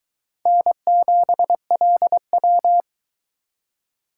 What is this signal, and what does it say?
Morse code, heard as a keyed tone, about 22 wpm, 700 Hz. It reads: N7LW